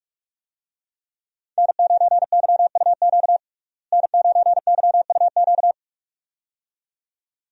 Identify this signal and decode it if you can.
Morse code, heard as a keyed tone, about 44 wpm, 700 Hz. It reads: N9YUQ N9YUQ
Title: USA Call Signs - 5 Characters - Repeated 2x - 45wpm